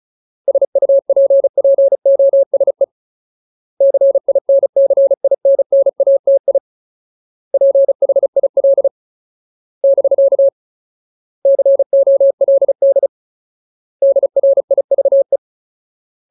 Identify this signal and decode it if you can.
Morse code, heard as a keyed tone, about 34 wpm, 550 Hz. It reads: SUPPOSE CINCINNATI PHIL <BK> COLD DRIVE
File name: Ultimate Word List / 35WPM Ultimate